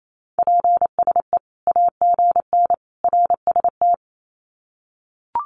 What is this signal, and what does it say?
Morse code, heard as a keyed tone, about 28 wpm, 700 Hz. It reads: PSE AGN RST
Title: Common QSO Phrases 28wpm